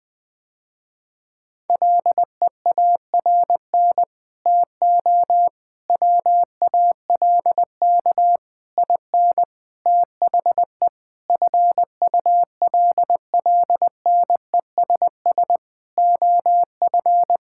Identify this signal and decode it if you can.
Morse code, heard as a keyed tone, about 21 wpm, 700 Hz. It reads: LEARN TO WALK IN THE FULLNESS OF
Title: Sentences with 9-Word Phrases - Spoken Beginning - 20wpm